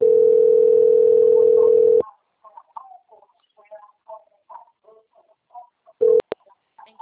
Ringback tone (North America), 440 + 480 Hz.